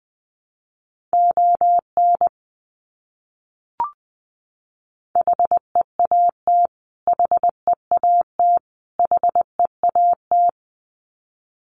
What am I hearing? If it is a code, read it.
Morse code, heard as a keyed tone, about 20 wpm, 700 Hz. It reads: ON HEAT HEAT HEAT
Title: Top 500 Words Repeated 20wpm